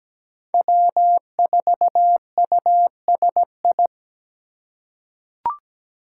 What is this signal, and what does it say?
Morse code, heard as a keyed tone, about 17 wpm, 700 Hz. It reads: W4USI
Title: K1USN SST QSO Practice - 17wpm